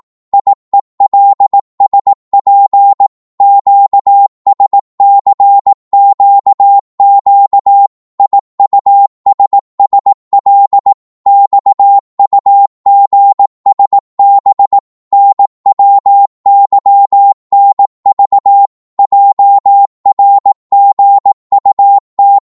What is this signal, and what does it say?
Morse code, heard as a keyed tone, about 18 wpm, 800 Hz. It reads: IELSP QSCQQ IUSSL XUGSB NWYNV JRGUT
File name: Morse / ltr-18wpm